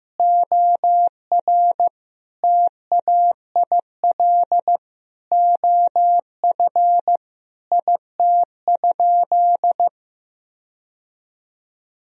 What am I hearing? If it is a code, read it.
Morse code, heard as a keyed tone, about 15 wpm, 700 Hz. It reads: OR TAIL OF IT?
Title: Sentences from Top 1000 Words - Review Part 3 - 15wpm